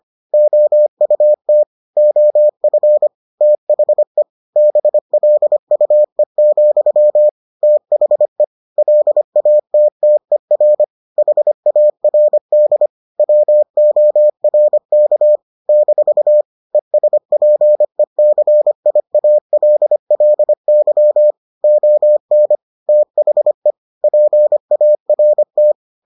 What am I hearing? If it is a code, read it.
Morse code, heard as a keyed tone, about 25 wpm, 600 Hz. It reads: OUT OF THE BLUE, THE LATTER HARD WORK - ESPECIALLY ON THE PART